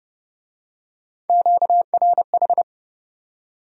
Morse code, heard as a keyed tone, about 25 wpm, 700 Hz. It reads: QRH
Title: Speed-Racing Q-Codes - 20wpm